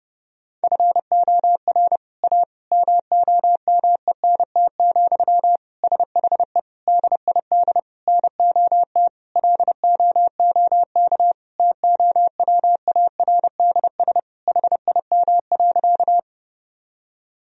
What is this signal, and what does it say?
Morse code, heard as a keyed tone, about 29 wpm, 700 Hz. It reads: FOR A MOMENT, SHE DID NOT LOOK TOWARDS HIM.
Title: Sentences from Top 1000 Words - Review Part 2 - 30wpm